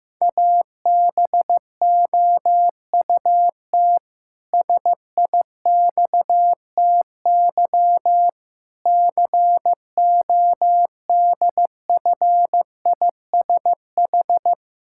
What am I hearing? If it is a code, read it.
Morse code, heard as a keyed tone, about 16 wpm, 700 Hz. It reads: ABOUT SIXTY CODFISH
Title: Pangram Challenge v2 - 15wpm